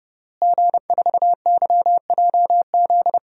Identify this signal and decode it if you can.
Morse code, heard as a keyed tone, about 30 wpm, 700 Hz. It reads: G4YJZ
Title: DXCC-100 Call Signs - 5 Characters - Repeated 3x - 30wpm